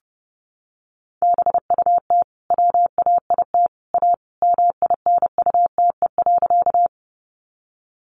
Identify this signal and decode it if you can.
Morse code, heard as a keyed tone, about 28 wpm, 700 Hz. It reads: BUT WAIT A MINUTE.
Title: Sentences from Top 500 Words 30wpm